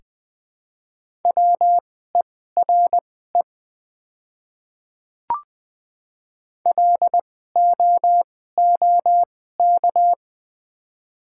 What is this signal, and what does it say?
Morse code, heard as a keyed tone, about 20 wpm, spaced as for 15 wpm, 690 Hz. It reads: WERE LOOK
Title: 2x Character Spacing - Top 100 Words - 20wpm